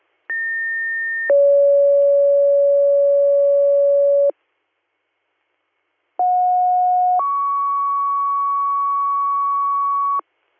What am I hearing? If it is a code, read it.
Two-tone page, heard as a one-second tone, then a three-second tone, about 1800 then 560 Hz, about 730 then 1120 Hz. It is heard from 0.3 to 10.2 s.